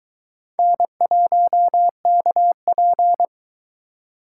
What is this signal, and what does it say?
Morse code, heard as a keyed tone, about 21 wpm, 700 Hz. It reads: N1KP